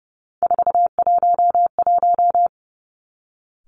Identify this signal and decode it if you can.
Morse code, heard as a keyed tone, about 30 wpm, 700 Hz. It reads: 411
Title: Three Challenging Letter-Number Combinations 30wpm